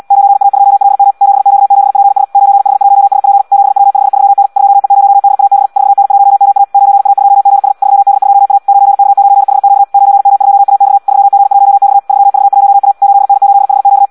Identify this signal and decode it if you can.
Morse code, heard as a keyed tone, about 74 wpm, 800 Hz. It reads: 9A2LM 7Q7PS 9H1SP 7W53N 2E0DAB 4N1MA 9A2GS 3D2A 9K9HQ 9M2AX 3G1X 3V1A 8N8HQ
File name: Audio practica CW / qrq-tx-78 2